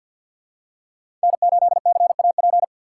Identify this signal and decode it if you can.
Morse code, heard as a keyed tone, about 51 wpm, 700 Hz. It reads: N8CAP
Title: USA Call Signs - 5 Characters - 4x Character Spacing - 50wpm